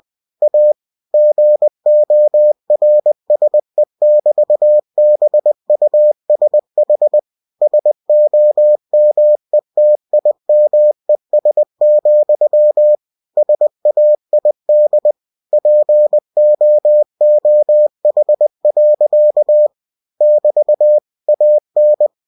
Morse code, heard as a keyed tone, about 20 wpm, 600 Hz. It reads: A GORSE<BT>BUSH SOMETIMES, SAID POOH. <BT> AN
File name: cw-books / winnie-the-pooh / winnie-20wpm-page-0082